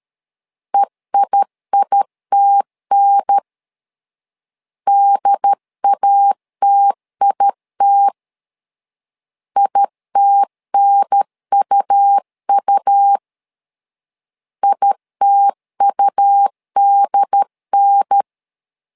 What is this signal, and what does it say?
Morse code, heard as a keyed tone, about 13 wpm, 800 Hz. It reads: EIITN DATIT ITNUU ITUDN
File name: LEZIONI / 2 UHDN / 2 UHDN 12wpm 1456